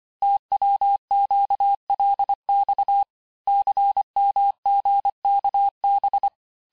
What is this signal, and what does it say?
Morse code, heard as a keyed tone, about 25 wpm, 800 Hz. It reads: TWQLX CMGKB
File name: texte-chiffre-vitesse-23mpm